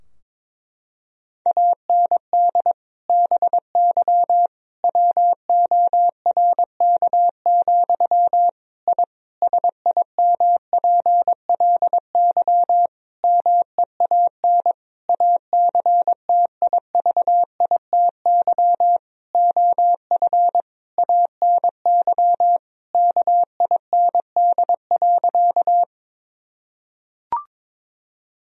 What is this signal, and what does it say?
Morse code, heard as a keyed tone, about 22 wpm, 700 Hz. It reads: AND BY WORK, I SIMPLY MEAN ACTIVITY OF ANY KIND.
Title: Sentences from Top 1000 Words - Review Part 4 - 22wpm